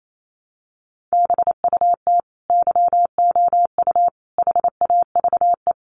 Morse code, heard as a keyed tone, about 28 wpm, 700 Hz. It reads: BUT YOU HAVE
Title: Sentences from Top 1100 Words 28wpm